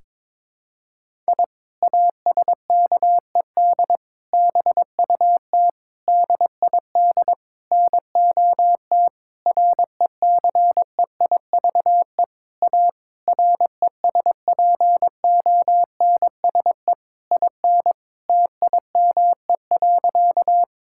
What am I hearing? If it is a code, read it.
Morse code, heard as a keyed tone, about 22 wpm, 700 Hz. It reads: I ASKED BUT DID NOT RECEIVE A RESPONSE IN TIME.
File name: practice / 22wpm / Sentences with 8-Word Phrases - 22wpm